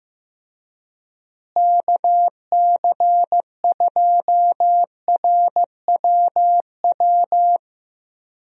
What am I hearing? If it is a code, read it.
Morse code, heard as a keyed tone, about 15 wpm, 700 Hz. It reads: KC2RWW
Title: USA Call Signs - 6 Characters - Repeated 2x - 15wpm